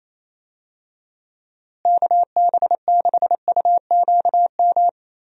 Morse code, heard as a keyed tone, about 28 wpm, 700 Hz. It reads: KB6UQM
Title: USA Call Signs - 6 Characters - Repeated 2x - 28wpm